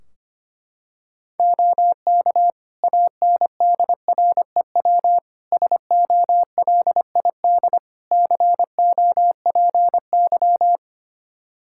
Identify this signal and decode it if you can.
Morse code, heard as a keyed tone, about 25 wpm, 700 Hz. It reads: OK ANDREW SOLID COPY
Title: Practice QSO - Exchange 3 - More Info - 25wpm